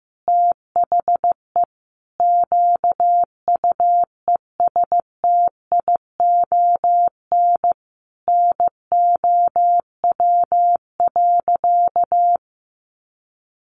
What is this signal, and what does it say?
Morse code, heard as a keyed tone, about 15 wpm, 700 Hz. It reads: THE QUESTION NOW.
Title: Sentences from Top 600 Words 15wpm